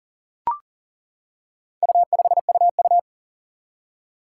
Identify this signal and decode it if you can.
Morse code, heard as a keyed tone, about 40 wpm, 700 Hz. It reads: U5UU
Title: Four Challenging Letter-Number Combinations 40wpm